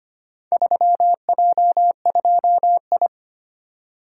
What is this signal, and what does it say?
Morse code, heard as a keyed tone, about 25 wpm, 700 Hz. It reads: 3J2I
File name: practice / 25wpm / Four Letter-Number Combinations - 25wpm